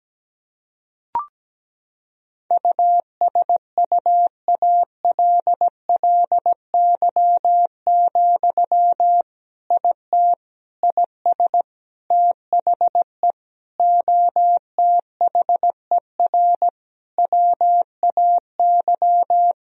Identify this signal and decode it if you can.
Morse code, heard as a keyed tone, about 17 wpm, 700 Hz. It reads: USUALLY, IT IS THE OTHER WAY
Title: Sentences from Top 1000 Words - Review Part 3 - 17wpm